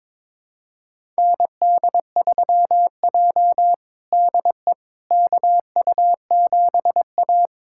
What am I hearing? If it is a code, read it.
Morse code, heard as a keyed tone, about 22 wpm, 700 Hz. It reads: ND3J DE KU7A
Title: USA Callsign DE Callsign - 4 Characters - 22wpm